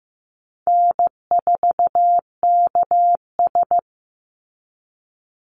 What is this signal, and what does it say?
Morse code, heard as a keyed tone, about 15 wpm, 700 Hz. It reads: N4KS
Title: Easy Call Signs - 15wpm